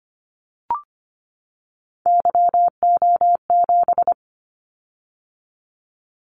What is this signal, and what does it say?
Morse code, heard as a keyed tone, about 25 wpm, 700 Hz. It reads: YO7
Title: Callsign Prefixes for Top 25 DXCC - 25wpm